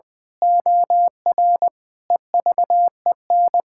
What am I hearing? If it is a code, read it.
Morse code, heard as a keyed tone, about 21 wpm, 700 Hz. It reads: OR EVEN
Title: Pangram Challenge v2 - 20wpm